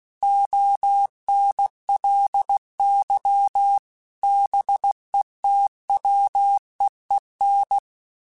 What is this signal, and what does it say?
Morse code, heard as a keyed tone, about 16 wpm, 800 Hz. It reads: ONLY BETWEEN
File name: Weather Sets / Weather Sets-mp3s / Weather 5 1 @15